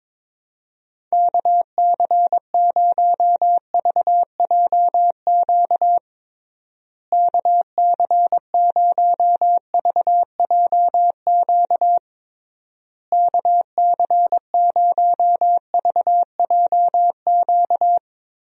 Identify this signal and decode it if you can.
Morse code, heard as a keyed tone, about 22 wpm, 700 Hz. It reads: KC0VJQ KC0VJQ KC0VJQ
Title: USA Call Signs - 6 Characters - Repeated 3x - 22wpm